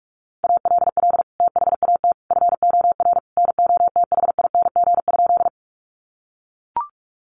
Morse code, heard as a keyed tone, about 44 wpm, 700 Hz. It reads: ALL THAT FOR NOTHING?
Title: Sentences from Top 500 Words 45wpm